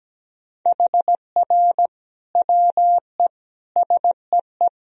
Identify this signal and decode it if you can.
Morse code, heard as a keyed tone, about 17 wpm, 700 Hz. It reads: HR WE SEE
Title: Sentences from Top 300 Words and Abbreviations 17wpm